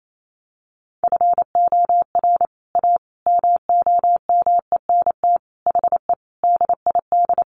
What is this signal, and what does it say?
Morse code, heard as a keyed tone, about 28 wpm, 700 Hz. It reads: FOR A MOMENT HE DID
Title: Sentences from Top 1100 Words 28wpm